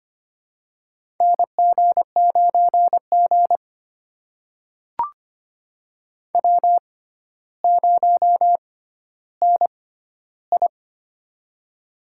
Morse code, heard as a keyed tone, about 25 wpm, spaced as for 13 wpm, 700 Hz. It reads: NG9G W0NI
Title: USA Call Signs - 4 Characters - 6x Character Spacing - 25wpm